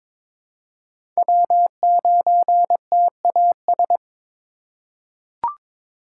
Morse code, heard as a keyed tone, about 21 wpm, 700 Hz. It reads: W9TAS
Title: USA Call Signs - 5 Characters - 4x Character Spacing - 22wpm